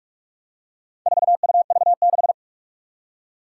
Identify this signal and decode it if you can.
Morse code, heard as a keyed tone, about 46 wpm, 700 Hz. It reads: 4UV6
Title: Four Challenging Letter-Number Combinations 45wpm